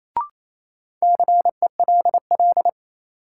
Morse code, heard as a keyed tone, about 28 wpm, 700 Hz. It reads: CELL
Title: Sentences from Top 1100 Words - New Word Drill Alt - 28wpm